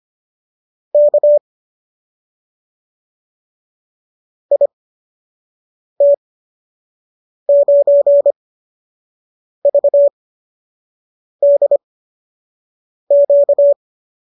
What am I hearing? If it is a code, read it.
Morse code, heard as a keyed tone, about 25 wpm, spaced as for 6 wpm, 590 Hz. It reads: K IT9VDQ